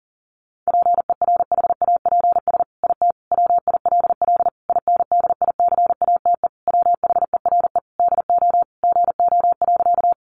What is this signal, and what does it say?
Morse code, heard as a keyed tone, about 40 wpm, 700 Hz. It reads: PERHAPS IT WILL INDICATE WHERE DO GO.
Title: Sentences from Top 1300 Words 40wpm